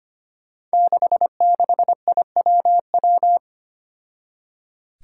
Morse code, heard as a keyed tone, about 25 wpm, 700 Hz. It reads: 66IWW
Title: Five Challenging Letter-Number Combinations 25wpm